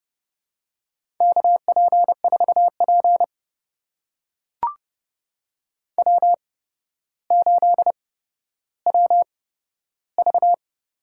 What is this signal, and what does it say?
Morse code, heard as a keyed tone, about 32 wpm, spaced as for 13 wpm, 700 Hz. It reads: KP4P W8WV
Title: USA Call Signs - 4 Characters - 8x Character Spacing - 30wpm